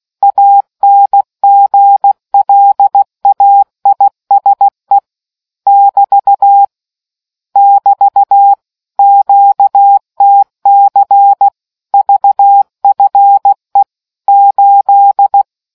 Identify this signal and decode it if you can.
Morse code, heard as a keyed tone, about 16 wpm, 800 Hz. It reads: ANGLAISE <BT> <BT> QTC VFE 8